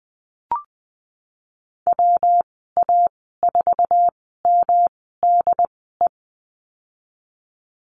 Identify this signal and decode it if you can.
Morse code, heard as a keyed tone, about 19 wpm, spaced as for 14 wpm, 700 Hz. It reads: WA4MDE
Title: 140 Lesson 23g - Letter f - Call Signs 20wpm